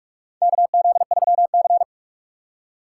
Morse code, heard as a keyed tone, about 43 wpm, 700 Hz. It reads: KZ3C
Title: USA Call Signs - 3x Character Spacing - 45wpm